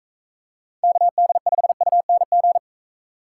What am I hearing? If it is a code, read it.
Morse code, heard as a keyed tone, about 42 wpm, 700 Hz. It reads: KD5UNG